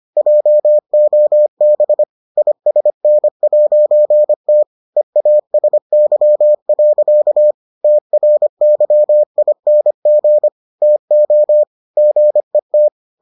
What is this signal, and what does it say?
Morse code, heard as a keyed tone, about 25 wpm, 600 Hz. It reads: JOB ISN'T EASY. TRYING TO GET